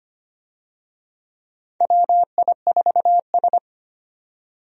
Morse code, heard as a keyed tone, about 25 wpm, 700 Hz. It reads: WI4S
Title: USA Call Signs - 4 Characters - 25wpm